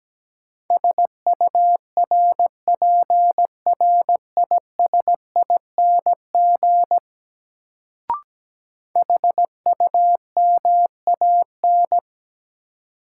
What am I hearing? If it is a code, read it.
Morse code, heard as a keyed tone, about 17 wpm, 700 Hz. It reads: SURPRISING HUMAN
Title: Sets of 5 Words - New Word Drill - 17wpm